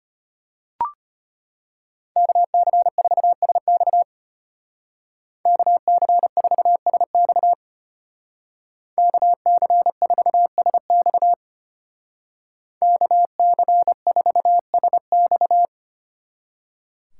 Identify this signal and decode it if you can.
Morse code, heard as a keyed tone, about 32 wpm, 700 Hz. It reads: KC4SX KC4SX KC4SX KC4SX